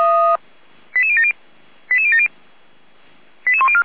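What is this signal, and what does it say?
Five-tone selective call (ZVEI), heard as quick runs of five tones, tones of about 70 ms. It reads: 88070 88070 88104
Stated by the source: ZVEI five-tone call received on SDR, recorded by Feuerwehr Orlamünde.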